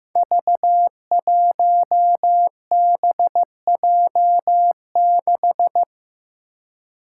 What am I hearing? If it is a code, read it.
Morse code, heard as a keyed tone, about 15 wpm, 700 Hz. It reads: V1BJ6
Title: Five Challenging Letter-Number Combinations 15wpm